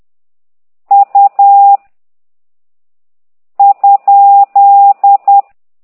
Morse code, heard as a keyed tone, about 10 wpm, 800 Hz. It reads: U ?
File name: ct2idw / exerc11